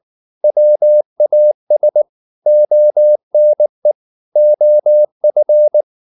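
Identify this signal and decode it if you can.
Morse code, heard as a keyed tone, about 19 wpm, 600 Hz. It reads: WAS ONE OF
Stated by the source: Morse code at 19 wpm from CW med Gnister, reading Ken Follett's novel Never in English.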